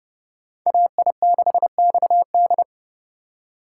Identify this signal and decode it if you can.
Morse code, heard as a keyed tone, about 30 wpm, 700 Hz. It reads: AI6XD
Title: USA Call Signs - 5 Characters - 4x Character Spacing - 30wpm